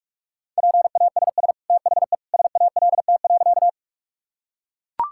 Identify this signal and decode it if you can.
Morse code, heard as a keyed tone, about 41 wpm, 690 Hz. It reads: PASS THE SALT.
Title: Sentences from Top 1000 Words - Review Part 5 - 45wpm